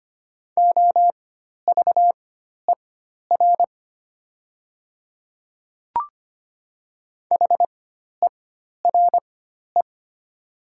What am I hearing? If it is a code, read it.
Morse code, heard as a keyed tone, about 25 wpm, spaced as for 12 wpm, 690 Hz. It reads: OVER HERE
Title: 4x Character Spacing - Top 200 Words - 25wpm